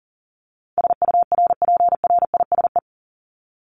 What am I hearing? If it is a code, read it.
Morse code, heard as a keyed tone, about 37 wpm, 700 Hz. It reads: SURPRISE
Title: Top 1751-2000 Words 40wpm